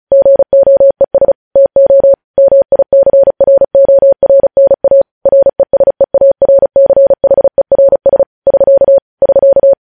Morse code, heard as a keyed tone, about 36 wpm, 550 Hz. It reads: GOES TO MICRORNA RESEARCHERS <SK> <SK>